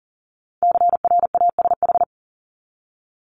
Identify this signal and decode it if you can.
Morse code, heard as a keyed tone, about 40 wpm, 700 Hz. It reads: CRASH
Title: Top 1751-2000 Words 40wpm